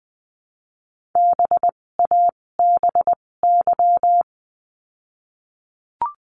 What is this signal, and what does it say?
Morse code, heard as a keyed tone, about 19 wpm, spaced as for 16 wpm, 700 Hz. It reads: BABY
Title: 256 Lesson 42a - Words 12-20wpm